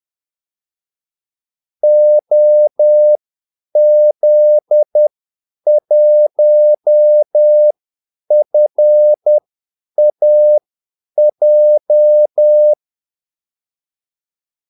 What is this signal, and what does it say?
Morse code, heard as a keyed tone, about 10 wpm, spaced as for 8 wpm, 600 Hz. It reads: OZ1FAJ
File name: Callsigns-10-8wpm